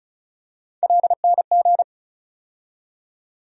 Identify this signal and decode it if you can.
Morse code, heard as a keyed tone, about 36 wpm, 700 Hz. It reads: LNG
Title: USA Call Sign Postfixes - 3 Letters 35wpm